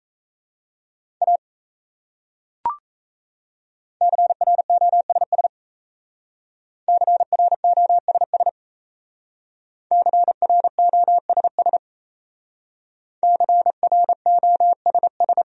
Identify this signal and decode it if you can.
Morse code, heard as a keyed tone, about 35 wpm, 700 Hz. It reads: A CROSS CROSS CROSS CROSS